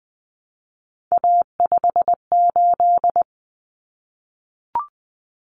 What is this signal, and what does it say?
Morse code, heard as a keyed tone, about 20 wpm, 700 Hz. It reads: A58
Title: Common Cut Numbers - 3-Digit 20wpm